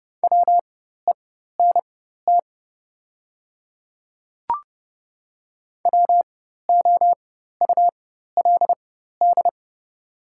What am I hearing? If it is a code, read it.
Morse code, heard as a keyed tone, about 31 wpm, spaced as for 14 wpm, 690 Hz. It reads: WENT WOULD
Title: 4x Character Spacing - Top 500 Words - 30wpm